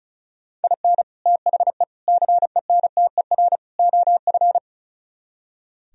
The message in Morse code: IN THE CENTER OF

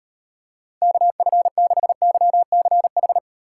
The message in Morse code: KF6YCH

